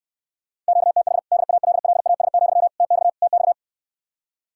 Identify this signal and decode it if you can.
Morse code, heard as a keyed tone, about 69 wpm, 700 Hz. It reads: QTH DALLAS, TX TX